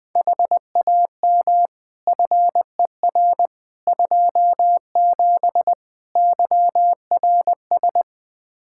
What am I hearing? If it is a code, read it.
Morse code, heard as a keyed tone, about 20 wpm, 700 Hz. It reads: HAM FER 27 YRS
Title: Practice QSO - Exchange 3 - More Info - 20wpm